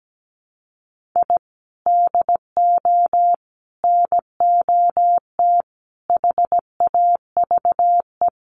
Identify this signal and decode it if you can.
Morse code, heard as a keyed tone, about 17 wpm, 700 Hz. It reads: I DO NOT HAVE
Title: Sets of 5 Words Encore 17wpm